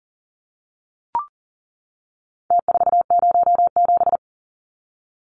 Morse code, heard as a keyed tone, about 40 wpm, 700 Hz. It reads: T407
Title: Numbers - Contest Style - 4-Digit 40wpm